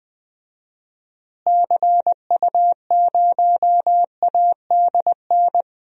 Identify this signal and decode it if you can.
Morse code, heard as a keyed tone, about 21 wpm, 700 Hz. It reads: CU0ADN
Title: DXCC-100 Call Signs - 6 Characters - Repeated 3x - 20wpm